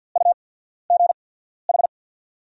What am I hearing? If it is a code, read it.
Morse code, heard as a keyed tone, about 48 wpm, spaced as for 14 wpm, 690 Hz. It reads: UGH
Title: 8x Character Spacing - Top 200 Words - 50wpm